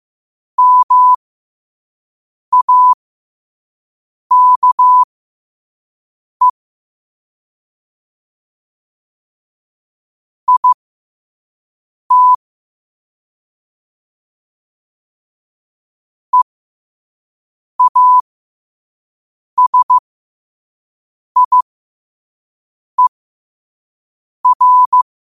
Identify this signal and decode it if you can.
Morse code, heard as a keyed tone, about 15 wpm, spaced as for 5 wpm, 1000 Hz. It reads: MAKE IT EASIER
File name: qotdfiles / 5wpm